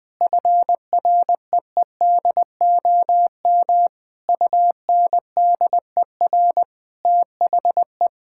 Morse code, heard as a keyed tone, about 20 wpm, 700 Hz. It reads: FREEDOM UNDER THE
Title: Sentences from Top 1000 Words - Review Part 6 - 20wpm